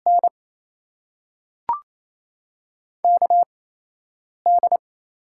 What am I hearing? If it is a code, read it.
Morse code, heard as a keyed tone, about 28 wpm, spaced as for 4 wpm, 700 Hz. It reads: NKD
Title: USA Call Signs - 6 Characters - 8x Character Spacing - 28wpm